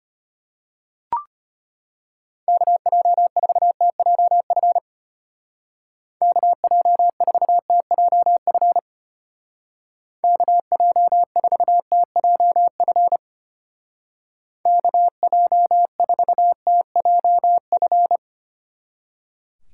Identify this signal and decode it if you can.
Morse code, heard as a keyed tone, about 30 wpm, 700 Hz. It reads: KJ4TJF KJ4TJF KJ4TJF KJ4TJF
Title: Speed Racing - USA Call Signs 25wpm